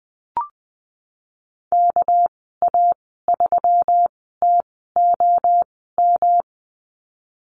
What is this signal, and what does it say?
Morse code, heard as a keyed tone, about 20 wpm, spaced as for 15 wpm, 700 Hz. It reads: KA3TOM
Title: 213 Lesson 34g - Number 8 - Call Signs 20wpm